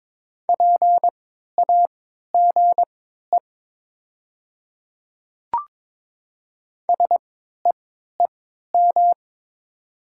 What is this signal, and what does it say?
Morse code, heard as a keyed tone, about 22 wpm, spaced as for 12 wpm, 700 Hz. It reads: PAGE SEEM
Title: 3x Character Spacing - Top 300 Words - 22wpm